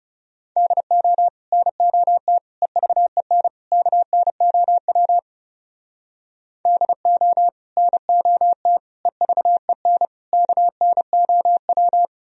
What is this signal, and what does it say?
Morse code, heard as a keyed tone, about 40 wpm, 700 Hz. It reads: DO NOT EVEN KNOW DO NOT EVEN KNOW